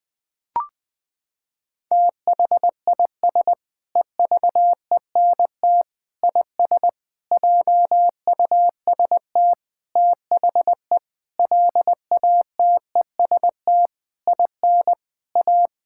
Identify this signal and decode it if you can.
Morse code, heard as a keyed tone, about 21 wpm, 700 Hz. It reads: THIS EVENT IS JUST THE LATEST IN A
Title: Sentences with 8-Word Phrases - 20wpm